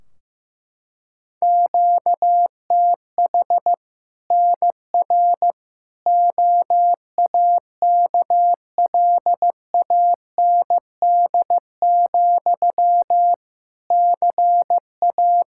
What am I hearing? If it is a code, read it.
Morse code, heard as a keyed tone, about 15 wpm, 700 Hz. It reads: QTH NR OAKLAND, CA
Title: K1USN SST QSO Practice 15wpm